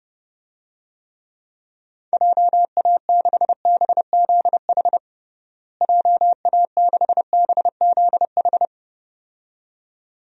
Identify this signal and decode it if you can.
Morse code, heard as a keyed tone, about 30 wpm, 700 Hz. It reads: JA6BZH JA6BZH
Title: DXCC-100 Call Signs - 6 Characters - Repeated 2x - 30wpm